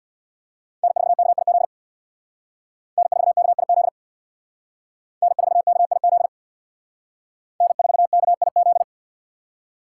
Morse code, heard as a keyed tone, about 60 wpm, 700 Hz. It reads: N4KIZ N4KIZ N4KIZ N4KIZ
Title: Speed Racing - USA 1x3 Call Signs 50wpm